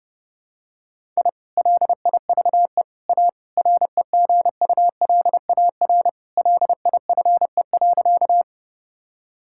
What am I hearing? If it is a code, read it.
Morse code, heard as a keyed tone, about 30 wpm, 700 Hz. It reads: I LIVE A REGULAR LIFE.
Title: Sentences from Top 1000 Words - Review Part 6 - 30wpm